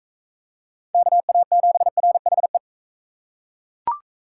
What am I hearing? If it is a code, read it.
Morse code, heard as a keyed tone, about 39 wpm, 700 Hz. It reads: KA7RHE